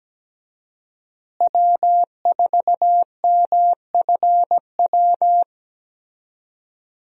Morse code, heard as a keyed tone, about 17 wpm, 700 Hz. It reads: W4MFW
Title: USA Call Signs - 5 Characters - Repeated 2x - 17wpm